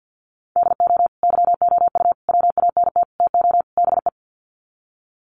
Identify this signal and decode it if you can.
Morse code, heard as a keyed tone, about 50 wpm, 700 Hz. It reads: DO YOU WANT TO BE
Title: Sets of 5 Words Encore2 50wpm